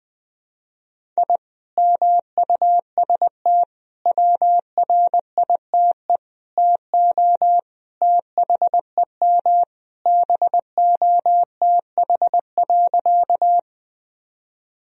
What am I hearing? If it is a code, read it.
Morse code, heard as a keyed tone, about 20 wpm, 700 Hz. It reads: I MUST WRITE TO THEM BOTH.
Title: Sentences from Top 1000 Words - Review Part 1 - 20wpm